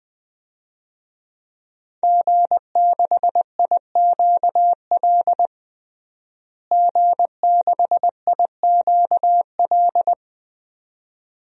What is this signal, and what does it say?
Morse code, heard as a keyed tone, about 20 wpm, 700 Hz. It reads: G6IQL G6IQL
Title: DXCC-100 Call Signs - 5 Characters - Repeated 2x - 20wpm